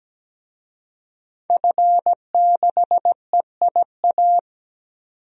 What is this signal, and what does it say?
Morse code, heard as a keyed tone, about 17 wpm, 700 Hz. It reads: F6EIA
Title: DXCC-100 Call Signs - 5 Characters - 17wpm